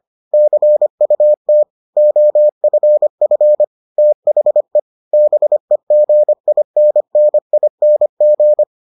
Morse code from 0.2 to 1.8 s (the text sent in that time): CUT